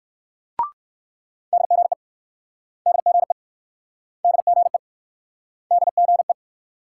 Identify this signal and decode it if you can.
Morse code, heard as a keyed tone, about 64 wpm, 700 Hz. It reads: DGE DGE DGE DGE